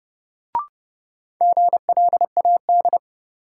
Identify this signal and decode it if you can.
Morse code, heard as a keyed tone, about 30 wpm, 700 Hz. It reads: GLAD
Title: Sentences from Top 800 Words - New Word Drill - 30wpm